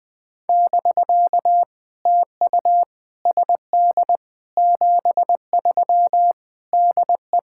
Morse code, heard as a keyed tone, about 21 wpm, 700 Hz. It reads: <BK> TU SD 73 DE
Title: POTA QSO Practice - 20wpm